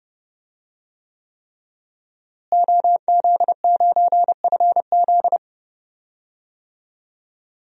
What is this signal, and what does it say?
Morse code, heard as a keyed tone, about 30 wpm, 700 Hz. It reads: OZ9FZ
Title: DXCC-100 Call Signs - 5 Characters - Repeated 3x - 30wpm